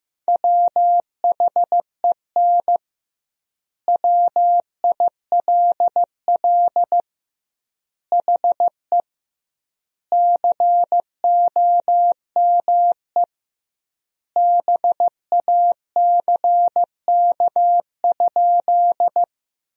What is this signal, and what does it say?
Morse code, heard as a keyed tone, about 16 wpm, 700 Hz. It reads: WHEN WILL HE COME BACK?
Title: 2x Word-Spacing - Sentences from Top 200 Words - 15wpm